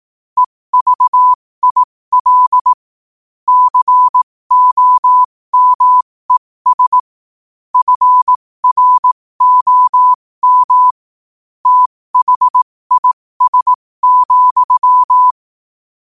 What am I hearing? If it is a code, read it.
Morse code, heard as a keyed tone, about 18 wpm, spaced as for 15 wpm, 1000 Hz. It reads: EVIL COMES FROM THIS,
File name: qotdfiles / 15wpm